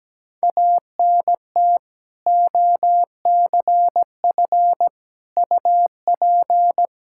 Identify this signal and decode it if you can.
Morse code, heard as a keyed tone, about 17 wpm, 700 Hz. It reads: ANT OCF UP
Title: Practice QSO - Exchange 2 - Info - 17wpm